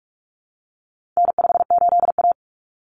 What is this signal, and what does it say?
Morse code, heard as a keyed tone, about 44 wpm, 700 Hz. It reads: N58A
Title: Common Cut Numbers - 4-Digit 45wpm